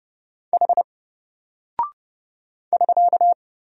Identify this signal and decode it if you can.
Morse code, heard as a keyed tone, about 30 wpm, 700 Hz. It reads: H <SK>
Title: Single Character (Letter:Number:Punctuation:Prosign) Alt 30wpm